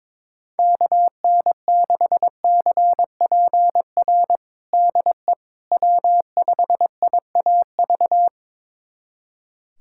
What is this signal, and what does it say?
Morse code, heard as a keyed tone, about 22 wpm, 700 Hz. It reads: KN6CPR DE W5IAV
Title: USA Callsign DE Callsign 22wpm